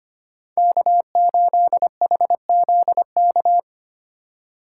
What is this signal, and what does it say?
Morse code, heard as a keyed tone, about 25 wpm, 700 Hz. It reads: K8HZK
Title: USA Call Signs - 5 Characters - 2x Character Spacing - 25wpm